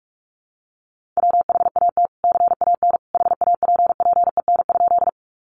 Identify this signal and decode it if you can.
Morse code, heard as a keyed tone, about 44 wpm, 700 Hz. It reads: WHAT CAN HAPPEN?
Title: Sentences from Top 400 Words 45wpm